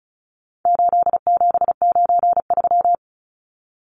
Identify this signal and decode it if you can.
Morse code, heard as a keyed tone, about 34 wpm, 700 Hz. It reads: 8793
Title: Numbers - 4-Digit 35wpm